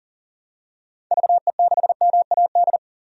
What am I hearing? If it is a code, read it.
Morse code, heard as a keyed tone, about 40 wpm, 700 Hz. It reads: VE6MAD
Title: DXCC-100 Call Signs - 6 Characters - 40wpm